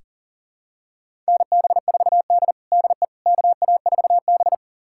Morse code, heard as a keyed tone, about 40 wpm, 700 Hz. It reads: NB4D DE KA4B
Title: USA Callsign DE Callsign - 4 Characters - 40wpm